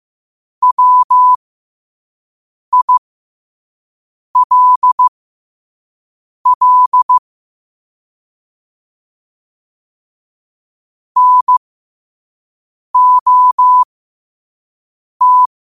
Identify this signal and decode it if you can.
Morse code, heard as a keyed tone, about 15 wpm, spaced as for 5 wpm, 1000 Hz. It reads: WILL NOT
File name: qotdfiles / 5wpm